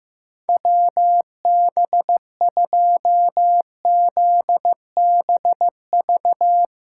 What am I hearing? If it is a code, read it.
Morse code, heard as a keyed tone, about 15 wpm, 700 Hz. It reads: WB2ZBV
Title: USA Call Signs - 6x Character Spacing - 15wpm